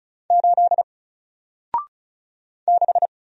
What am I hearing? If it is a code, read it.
Morse code, heard as a keyed tone, about 36 wpm, 700 Hz. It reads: 8 6
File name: Single-Letter-Number-35wpm